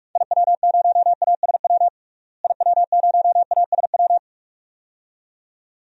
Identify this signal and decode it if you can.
Morse code, heard as a keyed tone, about 42 wpm, 700 Hz. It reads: IW0ASW IW0ASW
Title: DXCC-100 Call Signs - 6 Characters - Repeated 2x - 45wpm